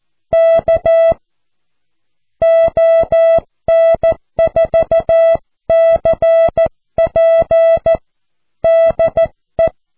Morse code, heard as a keyed tone, about 14 wpm, 660 Hz. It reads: K ON4CP DE
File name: cw / QSO